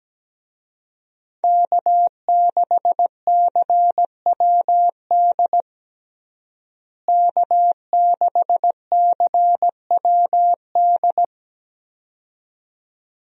Morse code, heard as a keyed tone, about 17 wpm, 700 Hz. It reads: K6CWD K6CWD
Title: USA Call Signs - 5 Characters - Repeated 2x - 17wpm